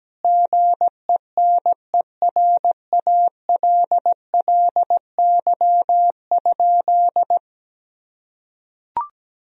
Morse code, heard as a keyed tone, about 17 wpm, 700 Hz. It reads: GENERALLY?
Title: Sentences from Top 1000 Words - Review Part 4 - 17wpm